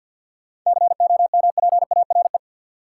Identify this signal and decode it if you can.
Morse code, heard as a keyed tone, about 53 wpm, 700 Hz. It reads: COMPARE
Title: Sentences from Top 1300 Words - New Word Drill Alt - 50wpm